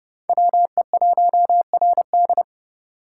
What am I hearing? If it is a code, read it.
Morse code, heard as a keyed tone, about 29 wpm, 700 Hz. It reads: WE1RD